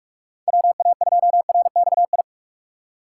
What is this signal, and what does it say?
Morse code, heard as a keyed tone, about 46 wpm, 700 Hz. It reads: WA2RXI
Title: USA Call Signs - 6 Characters - 6x Character Spacing - 45wpm